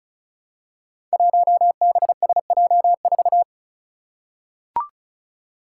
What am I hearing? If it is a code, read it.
Morse code, heard as a keyed tone, about 36 wpm, 700 Hz. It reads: 1BSJ4
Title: Five Challenging Letter-Number Combinations 35wpm